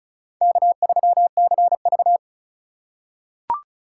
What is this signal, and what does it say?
Morse code, heard as a keyed tone, about 36 wpm, 700 Hz. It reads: K3CV
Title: USA Call Signs - 4 Characters - 3x Character Spacing - 35wpm